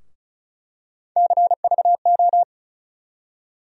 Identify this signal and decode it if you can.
Morse code, heard as a keyed tone, about 36 wpm, 700 Hz. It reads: CVO
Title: USA Call Sign Postfixes - 3 Letters 35wpm